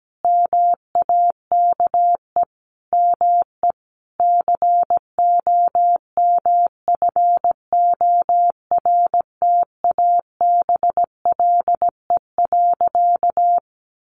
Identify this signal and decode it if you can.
Morse code, heard as a keyed tone, about 17 wpm, 700 Hz. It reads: MAKE ME COMFORTABLE.
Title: Sentences from Top 1500 Words 17wpm